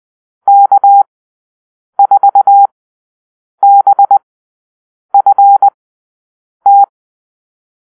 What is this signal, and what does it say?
Morse code, heard as a keyed tone, about 20 wpm, spaced as for 7 wpm, 800 Hz. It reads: K4BFT
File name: arrl / Call-Sign Drill-10